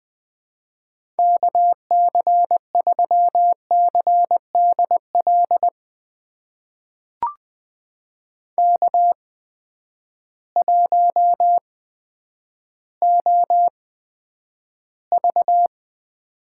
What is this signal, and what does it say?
Morse code, heard as a keyed tone, about 20 wpm, spaced as for 10 wpm, 700 Hz. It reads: KC3CDL K1OV
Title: USA Call Signs - 8x Character Spacing - 20wpm